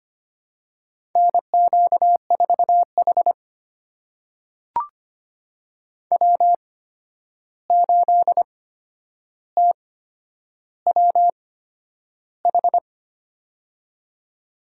Morse code, heard as a keyed tone, about 25 wpm, spaced as for 10 wpm, 700 Hz. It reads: NQ4H W8TWH
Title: USA Call Signs - 8x Character Spacing - 25wpm